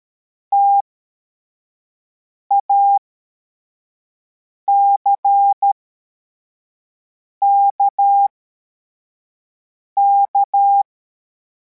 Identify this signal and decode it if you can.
Morse code, heard as a keyed tone, about 13 wpm, spaced as for 4 wpm, 800 Hz. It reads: TACKK